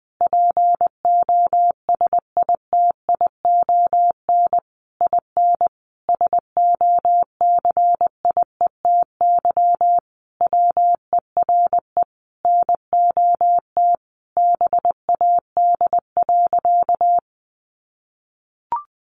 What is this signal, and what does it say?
Morse code, heard as a keyed tone, about 20 wpm, 700 Hz. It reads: POSITION IN SOCIETY WERE NOT BAD.
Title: Sentences from Top 900 Words 20wpm